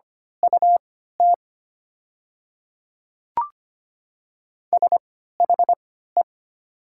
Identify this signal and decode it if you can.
Morse code, heard as a keyed tone, about 25 wpm, spaced as for 14 wpm, 700 Hz. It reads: UT SHE
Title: 3x Character Spacing - Top 100 Words - 25wpm